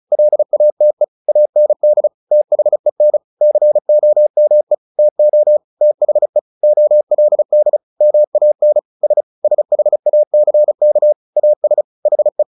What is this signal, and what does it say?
Morse code, heard as a keyed tone, about 34 wpm, 600 Hz. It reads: LATE AND THEN COME TO THE OLD MAN S SHACK AS HE
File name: the old man and the sea 2 35wpm